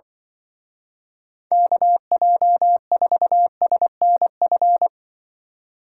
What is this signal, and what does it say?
Morse code, heard as a keyed tone, about 23 wpm, 700 Hz. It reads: KJ4SNF
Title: Speed Racing - USA 2x3 Call Signs 20wpm